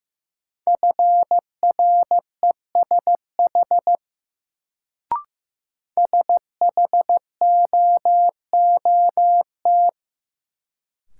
Morse code, heard as a keyed tone, about 15 wpm, 700 Hz. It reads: FRESH SHOOT
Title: Sentences from Top 1200 Words - New Word Drill - 15wpm